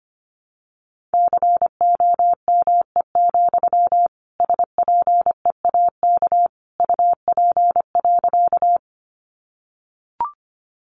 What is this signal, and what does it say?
Morse code, heard as a keyed tone, about 25 wpm, 700 Hz. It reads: COME, SPEAK UP.
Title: Sentences from Top 1100 Words 25wpm